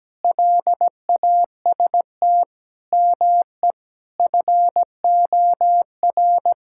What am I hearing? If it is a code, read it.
Morse code, heard as a keyed tone, about 17 wpm, 700 Hz. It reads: LAST ME FOR
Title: Sentences from Top 1000 Words - Review Part 6 - 17wpm